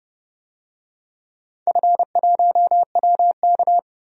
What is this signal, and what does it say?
Morse code, heard as a keyed tone, about 31 wpm, 700 Hz. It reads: F1WK
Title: DXCC-100 Call Signs - 4 Characters - 30wpm